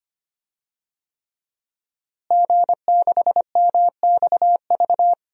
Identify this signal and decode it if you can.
Morse code, heard as a keyed tone, about 25 wpm, 700 Hz. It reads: G6MXV
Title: DXCC-100 Call Signs - 5 Characters - Repeated 3x - 25wpm